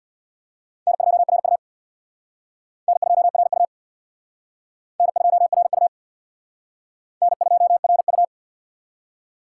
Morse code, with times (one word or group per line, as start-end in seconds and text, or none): 0.9-1.6 N2RU
2.9-3.7 N2RU
5.0-5.9 N2RU
7.2-8.2 N2RU